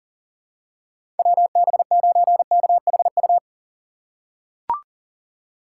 Morse code, heard as a keyed tone, about 40 wpm, 700 Hz. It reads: WB9KHU
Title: USA Call Signs - 6 Characters - 4x Character Spacing - 40wpm